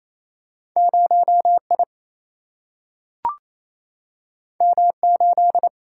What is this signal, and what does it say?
Morse code, heard as a keyed tone, about 28 wpm, 700 Hz. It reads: 0 I M 8
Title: Single-Letter Single-Number 28wpm